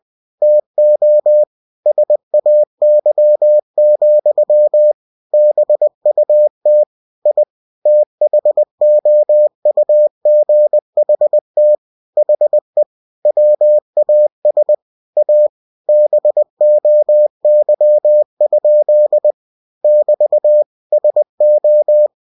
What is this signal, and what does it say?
Morse code, heard as a keyed tone, about 20 wpm, 600 Hz. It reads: TO SAY, BUT I THOUGHT HE WAS A BOY? <BT> SO
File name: cw-books / winnie-the-pooh / winnie-20wpm-page-0000